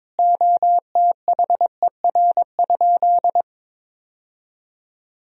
Morse code, heard as a keyed tone, about 22 wpm, 700 Hz. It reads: OTHER?
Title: Sentences from Top 1000 Words - Review Part 1 - 22wpm